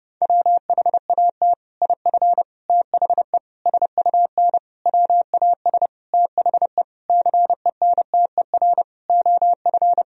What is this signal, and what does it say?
Morse code, heard as a keyed tone, about 30 wpm, 700 Hz. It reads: WHAT IF THE SUN WAS THE CENTER OF
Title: Sentences from Top 1000 Words - Review Part 2 - 30wpm